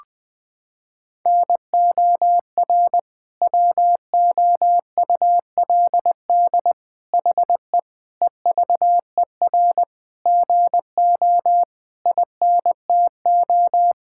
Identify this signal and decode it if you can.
Morse code, heard as a keyed tone, about 20 wpm, 700 Hz. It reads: NOR WOULD HE EVER GO INTO
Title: Sentences from Top 1600 Words 20wpm